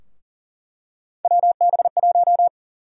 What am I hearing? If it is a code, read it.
Morse code, heard as a keyed tone, about 40 wpm, 700 Hz. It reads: WB1
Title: USA Call Sign Prefixes - Repeated 2x - 40wpm